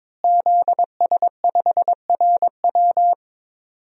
Morse code, heard as a keyed tone, about 21 wpm, 700 Hz. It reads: ZS5RW
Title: DXCC-100 Call Signs - 5 Characters - Repeated 3x - 22wpm